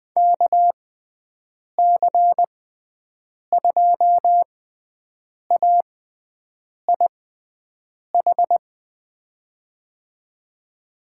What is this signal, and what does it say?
Morse code, heard as a keyed tone, about 20 wpm, spaced as for 7 wpm, 700 Hz. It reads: KC2AIH
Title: USA Call Signs - 6 Characters - 6x Character Spacing - 20wpm